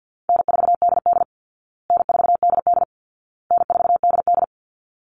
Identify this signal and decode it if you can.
Morse code, heard as a keyed tone, about 50 wpm, 700 Hz. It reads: N4DD N4DD N4DD
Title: Call Signs Repeated 2x - 50wpm